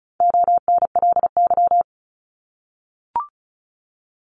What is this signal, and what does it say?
Morse code, heard as a keyed tone, about 34 wpm, 700 Hz. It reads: ONLY
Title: Top 200 Words 35wpm